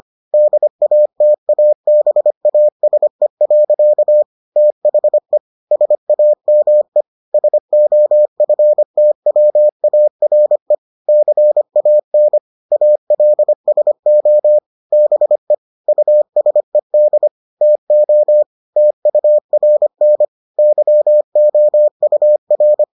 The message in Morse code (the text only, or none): DATABASE. THE SAME SOFTWARE CAN ALSO BE USED TO TURN YOUR